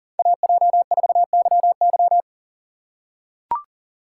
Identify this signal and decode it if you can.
Morse code, heard as a keyed tone, about 39 wpm, 700 Hz. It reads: AJ4YY
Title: USA Call Signs - 5 Characters - 4x Character Spacing - 40wpm